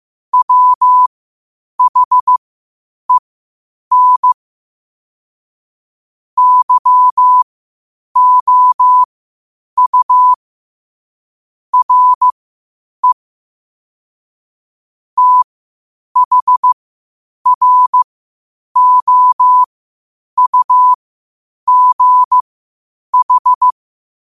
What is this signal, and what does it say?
Morse code, heard as a keyed tone, about 15 wpm, spaced as for 8 wpm, 1000 Hz. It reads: WHEN YOU RE THROUGH
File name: qotdfiles / 8wpm